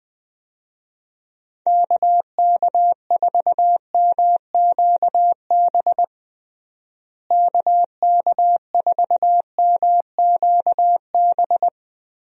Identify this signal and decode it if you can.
Morse code, heard as a keyed tone, about 20 wpm, 700 Hz. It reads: KK4MQB KK4MQB
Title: USA Call Signs - 6 Characters - Repeated 2x - 20wpm